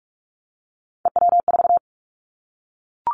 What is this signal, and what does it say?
Morse code, heard as a keyed tone, about 46 wpm, 700 Hz. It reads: EW4
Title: Callsign Prefixes for Top 100 DXCC - 45wpm